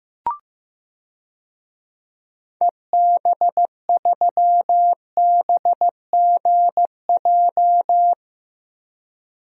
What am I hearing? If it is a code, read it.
Morse code, heard as a keyed tone, about 15 wpm, 700 Hz. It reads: EB3BGJ
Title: DXCC-100 Call Signs - 6 Characters - Repeated 3x - 15wpm